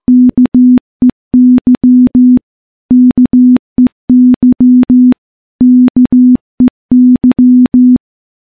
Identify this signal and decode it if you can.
Morse code, heard as a keyed tone, about 15 wpm, 250 Hz. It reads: KEY KEY KEY
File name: morse / morse-training-15WPM